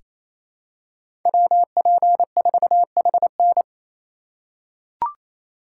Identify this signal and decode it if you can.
Morse code, heard as a keyed tone, about 28 wpm, 700 Hz. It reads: WP4HN
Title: USA Call Signs - 5 Characters - 8x Character Spacing - 28wpm